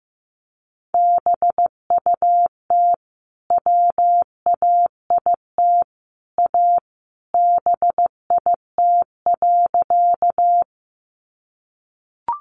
Morse code, heard as a keyed tone, about 15 wpm, 700 Hz. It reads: BUT WAIT A BIT.
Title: Sentences from Top 900 Words 15wpm